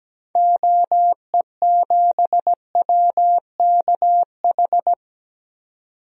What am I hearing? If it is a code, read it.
Morse code, heard as a keyed tone, about 17 wpm, 700 Hz. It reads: OE7WKH
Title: DXCC-100 Call Signs - 6 Characters - 17wpm